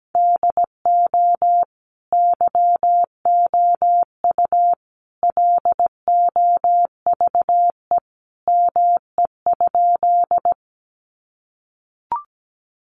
Morse code, heard as a keyed tone, about 17 wpm, 700 Hz. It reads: DO YOU LOVE ME?
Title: Sentences from Top 400 Words 17wpm